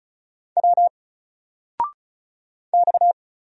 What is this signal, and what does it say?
Morse code, heard as a keyed tone, about 36 wpm, 700 Hz. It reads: W X
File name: Single-Letter-Number-35wpm